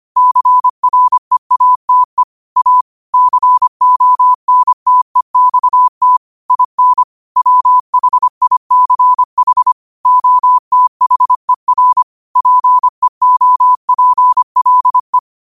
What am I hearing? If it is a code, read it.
Morse code, heard as a keyed tone, about 25 wpm, 1000 Hz. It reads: CREATE A CONTEXT IN WHICH OTHER PEOPLE